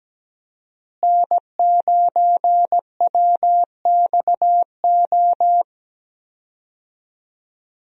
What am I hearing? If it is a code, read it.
Morse code, heard as a keyed tone, about 17 wpm, 700 Hz. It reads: N9WXO
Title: USA Call Signs - 5 Characters - Repeated 3x - 17wpm